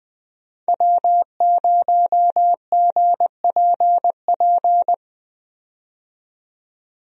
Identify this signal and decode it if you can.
Morse code, heard as a keyed tone, about 21 wpm, 700 Hz. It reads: W0GPP
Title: USA Call Signs - 5 Characters - 20wpm